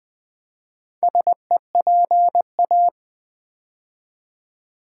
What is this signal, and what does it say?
Morse code, heard as a keyed tone, about 20 wpm, 700 Hz. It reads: SEPA
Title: Build A Word - 4 Letters At A Time - 10-Letter Words - 20wpm